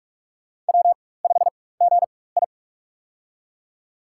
Morse code, heard as a keyed tone, about 45 wpm, spaced as for 21 wpm, 700 Hz. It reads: W5GI
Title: USA Call Signs - 4 Characters - 4x Character Spacing - 45wpm